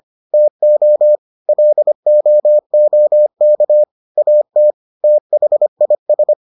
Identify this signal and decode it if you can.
Morse code, heard as a keyed tone, about 25 wpm, 600 Hz. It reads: TO LOOK AT THIS